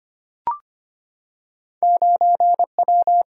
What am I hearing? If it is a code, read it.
Morse code, heard as a keyed tone, about 25 wpm, 700 Hz. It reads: 9W
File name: Two Letter-Number Combinations - 25wpm